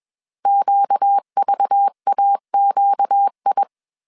Morse code, heard as a keyed tone, about 21 wpm, 800 Hz. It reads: QVAQI
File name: CW / DET / 1 6 1500C S B 20wpm 2048